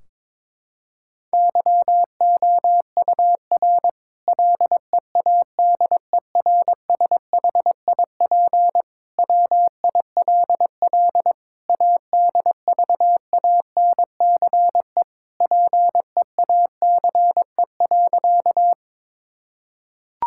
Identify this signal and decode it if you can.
Morse code, heard as a keyed tone, about 22 wpm, 700 Hz. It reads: YOUR LEADERSHIP WILL ADVANCE PEACE.
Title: Sentences from Top 1600 Words 22wpm